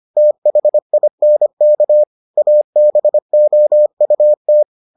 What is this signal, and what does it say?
Morse code, heard as a keyed tone, about 25 wpm, 600 Hz. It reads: THINK ABOUT